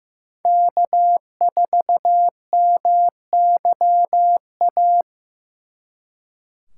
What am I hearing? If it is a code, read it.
Morse code, heard as a keyed tone, about 18 wpm, 700 Hz. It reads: K4MYA
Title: Speed Racing - USA 1x3 Call Signs 15wpm